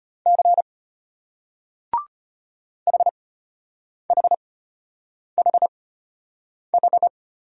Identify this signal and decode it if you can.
Morse code, heard as a keyed tone, about 34 wpm, spaced as for 8 wpm, 700 Hz. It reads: C HHHH